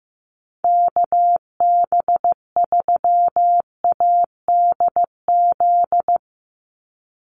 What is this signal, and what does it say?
Morse code, heard as a keyed tone, about 15 wpm, 700 Hz. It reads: KB3ADZ
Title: USA Call Signs 15wpm